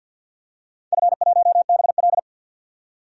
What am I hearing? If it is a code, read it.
Morse code, heard as a keyed tone, about 46 wpm, 700 Hz. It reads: F1BL
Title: DXCC-100 Call Signs - 4 Characters - 50wpm